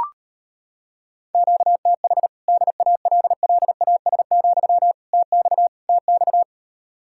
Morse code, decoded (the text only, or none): QTH DALLAS, TX TX